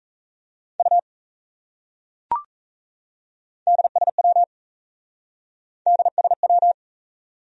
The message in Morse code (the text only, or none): U DSW DSW